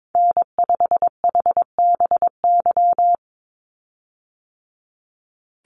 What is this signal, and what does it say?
Morse code, heard as a keyed tone, about 22 wpm, 700 Hz. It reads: N5HBY
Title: USA Call Signs 22wpm